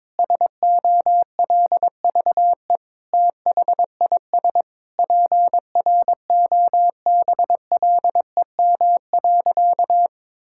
Morse code, heard as a keyed tone, about 21 wpm, 700 Hz. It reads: SOLVE THIS PROBLEM.
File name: Sentences with 7-Word Phrases - 22wpm